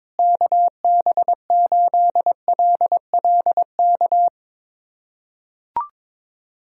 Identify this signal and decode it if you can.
Morse code, heard as a keyed tone, about 22 wpm, 700 Hz. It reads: KB8LLK
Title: USA Callsign DE Callsign 22wpm